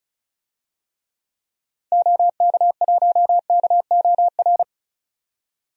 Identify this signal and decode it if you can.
Morse code, heard as a keyed tone, about 36 wpm, 700 Hz. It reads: OK1KOR
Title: DXCC-100 Call Signs - 6 Characters - 35wpm